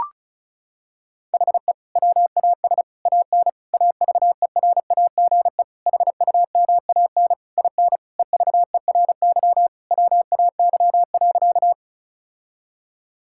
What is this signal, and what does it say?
Morse code, heard as a keyed tone, about 36 wpm, 700 Hz. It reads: HE WAS AN AVERAGE HUMAN IN EVERY WAY.
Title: Sentences from Top 1000 Words - Review Part 6 - 35wpm